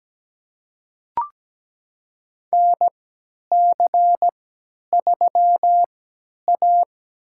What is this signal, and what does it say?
Morse code, heard as a keyed tone, about 17 wpm, spaced as for 10 wpm, 700 Hz. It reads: NC3A
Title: USA Call Signs - 4 Characters - 3x Character Spacing - 17wpm